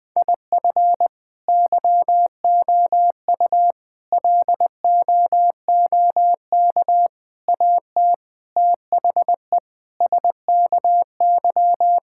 Morse code, heard as a keyed tone, about 20 wpm, 700 Hz. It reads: IF YOU LOOK AT THE SKY
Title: Sentences with 4-Word Phrases - 20wpm